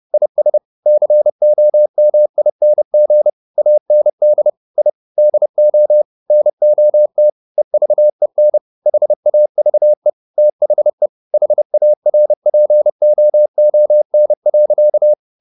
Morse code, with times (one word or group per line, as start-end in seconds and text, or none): 0.1-0.6 IS
0.9-3.3 COMING
3.6-4.5 AND
4.8-4.9 I
5.2-6.0 DO
6.3-7.3 NOT
7.6-8.6 EVEN
8.9-10.1 HAVE
10.4-11.1 THE
11.3-15.1 HARPOON.